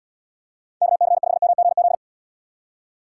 Morse code, heard as a keyed tone, about 75 wpm, 700 Hz. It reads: KC5MGZ